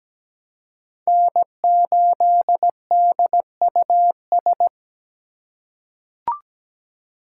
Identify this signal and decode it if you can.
Morse code, heard as a keyed tone, about 17 wpm, 700 Hz. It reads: N8DUS
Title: K1USN SST QSO Practice - 17wpm